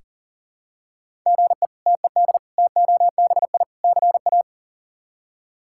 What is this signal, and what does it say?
Morse code, heard as a keyed tone, about 40 wpm, 700 Hz. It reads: GE TED TOBI CA
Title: K1USN SST QSO Practice - 40wpm